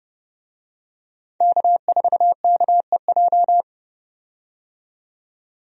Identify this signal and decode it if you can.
Morse code, heard as a keyed tone, about 29 wpm, 700 Hz. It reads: K4KEJ
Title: USA Call Signs - 5 Characters - 30wpm